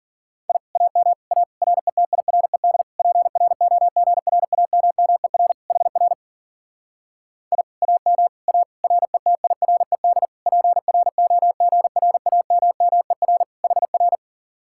Morse code, heard as a keyed tone, about 48 wpm, 700 Hz. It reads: I AM A RETIRED PROGRAMMER HR I AM A RETIRED PROGRAMMER HR
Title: Speed-Racing Common QSO Phrases Encore - 40wpm